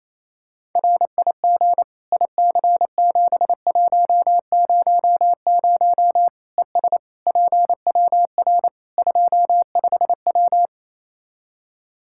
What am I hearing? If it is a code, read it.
Morse code, heard as a keyed tone, about 28 wpm, 700 Hz. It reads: RIG IC7100 ES PWR 25W
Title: K1USN SST QSO Practice 28wpm